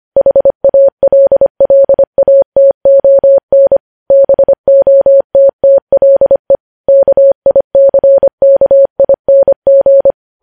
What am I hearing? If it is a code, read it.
Morse code, heard as a keyed tone, about 25 wpm, 550 Hz. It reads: HALLATON BOTTLE KICKING
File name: podcast / News headlines-1x-25x25